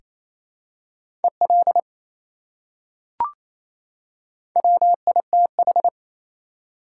Morse code, heard as a keyed tone, about 28 wpm, 700 Hz. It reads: EL WITH
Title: Top 180 QSO Elements 28wpm